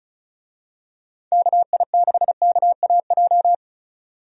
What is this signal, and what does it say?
Morse code, heard as a keyed tone, about 36 wpm, 700 Hz. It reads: KI6KAJ
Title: USA Call Signs - 6 Characters - 35wpm